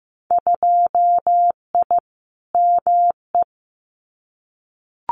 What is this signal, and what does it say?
Morse code, heard as a keyed tone, about 15 wpm, 700 Hz. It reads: 2I ME
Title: Winter Field Day Exchanges Encore 15wpm